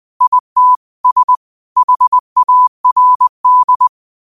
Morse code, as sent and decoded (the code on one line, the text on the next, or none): .. - / ... / .... .- .-. -..
IT S HARD